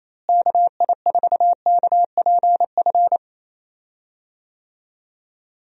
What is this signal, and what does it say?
Morse code, heard as a keyed tone, about 27 wpm, 700 Hz. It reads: KI4KPF
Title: USA Call Signs - 6 Characters - Repeated 3x - 28wpm